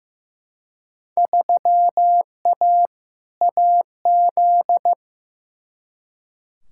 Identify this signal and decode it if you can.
Morse code, heard as a keyed tone, about 16 wpm, 700 Hz. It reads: 3A AZ
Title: Field Day Exchanges - 15wpm